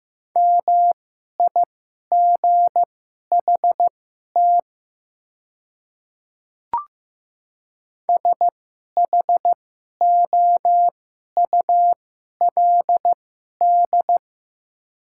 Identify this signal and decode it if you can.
Morse code, heard as a keyed tone, about 15 wpm, spaced as for 11 wpm, 700 Hz. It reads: MIGHT SHOULD
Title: 2x Character Spacing - Top 300 Words - 15wpm